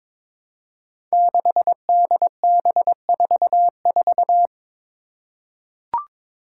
Morse code, heard as a keyed tone, about 22 wpm, 700 Hz. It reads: 6DB44
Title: Five Challenging Letter-Number Combinations 22wpm